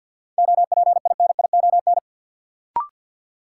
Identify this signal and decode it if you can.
Morse code, heard as a keyed tone, about 53 wpm, 700 Hz. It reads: OPINION
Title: Sentences from Top 1200 Words - New Word Drill Alt - 50wpm